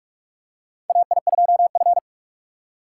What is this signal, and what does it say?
Morse code, heard as a keyed tone, about 46 wpm, 700 Hz. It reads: AI2F
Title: USA Call Signs - 4 Characters - 2x Character Spacing - 45wpm